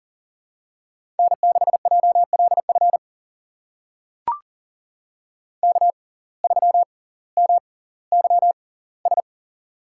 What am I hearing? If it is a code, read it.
Morse code, heard as a keyed tone, about 39 wpm, spaced as for 20 wpm, 700 Hz. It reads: N6JLF K3MYS
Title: USA Call Signs - 5 Characters - 6x Character Spacing - 40wpm